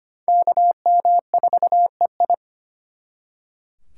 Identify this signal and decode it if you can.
Morse code, heard as a keyed tone, about 31 wpm, 700 Hz. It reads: KM4EI
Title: Speed Racing - USA 2x2 Call Signs 25wpm